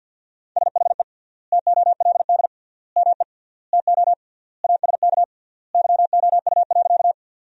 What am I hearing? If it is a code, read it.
Morse code, heard as a keyed tone, about 52 wpm, 700 Hz. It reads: SHE TOLD ME TO ASK YOU.
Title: 3x Word-Spacing - Sentences from Top 400 Words - 50wpm